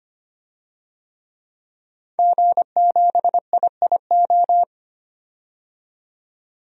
Morse code, heard as a keyed tone, about 25 wpm, 700 Hz. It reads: G7IIO
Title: DXCC-100 Call Signs - 5 Characters - 25wpm